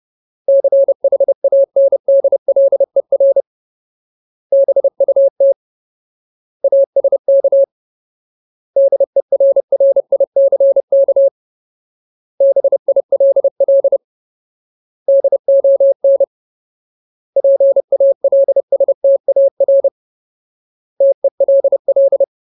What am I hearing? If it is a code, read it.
Morse code, heard as a keyed tone, about 30 wpm, 550 Hz. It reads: CHANDLER BUT ASK DERRICK BILL DON PALSTAR TELL
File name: Ultimate Word List / 30WPM Ultimate